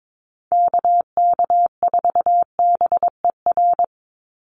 Morse code, heard as a keyed tone, about 22 wpm, 700 Hz. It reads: KK4BER
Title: USA Call Signs 22wpm